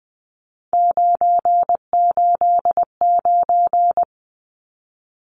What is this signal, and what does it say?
Morse code, heard as a keyed tone, about 20 wpm, 700 Hz. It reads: 989
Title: Numbers - 3-Digit 20wpm